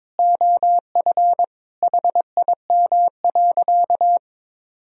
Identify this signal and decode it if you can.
Morse code, heard as a keyed tone, about 22 wpm, 700 Hz. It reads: OF HIM.
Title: Sentences from Top 1000 Words - Review Part 3 - 22wpm